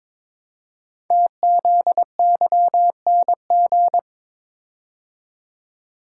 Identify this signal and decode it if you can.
Morse code, heard as a keyed tone, about 22 wpm, 700 Hz. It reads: TZYNG
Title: Code Groups - 5 Letter:Numbers - 22wpm